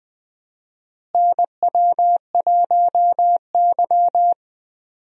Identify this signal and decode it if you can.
Morse code, heard as a keyed tone, about 20 wpm, 700 Hz. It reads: NW1Y
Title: USA Call Signs - 4 Characters - Repeated 2x - 20wpm